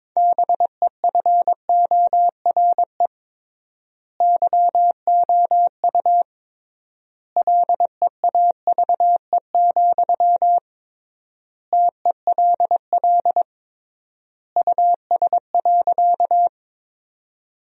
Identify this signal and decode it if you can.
Morse code, heard as a keyed tone, about 22 wpm, 700 Hz. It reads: BEFORE YOU LEAVE, TELL US.
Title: 3x Word-Spacing - Sentences from Top 400 Words - 22wpm